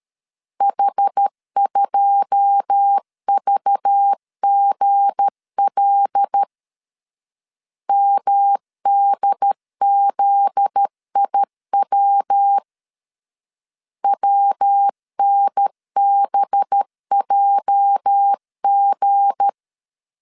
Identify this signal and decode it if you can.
Morse code, heard as a keyed tone, about 13 wpm, 800 Hz. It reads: H2VGL MDZIW WNBJG
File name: LEZIONI / 9 2 12wpm 2301